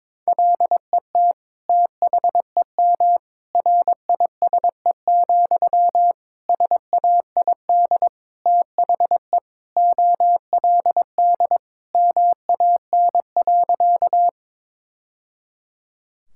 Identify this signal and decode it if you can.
Morse code, heard as a keyed tone, about 22 wpm, 700 Hz. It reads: LET THEM RISE, SAID THE OLD MAN.
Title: Sentences from Top 1000 Words - Review Part 6 - 22wpm